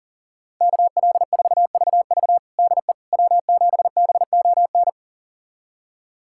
Morse code, heard as a keyed tone, about 40 wpm, 700 Hz. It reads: KL4VV DE W7BON
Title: USA Callsign DE Callsign - 5 Characters - 40wpm